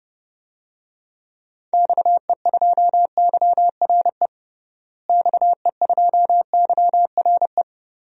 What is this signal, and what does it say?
Morse code, heard as a keyed tone, about 30 wpm, 700 Hz. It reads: XE2YRE XE2YRE
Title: DXCC-100 Call Signs - 6 Characters - Repeated 2x - 30wpm